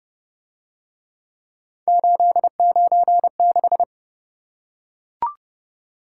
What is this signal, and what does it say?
Morse code, heard as a keyed tone, about 30 wpm, 700 Hz. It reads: 896